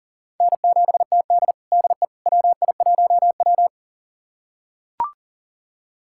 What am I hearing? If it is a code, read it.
Morse code, heard as a keyed tone, about 40 wpm, 700 Hz. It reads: N7TD DE WI1W
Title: USA Callsign DE Callsign - 4 Characters - 40wpm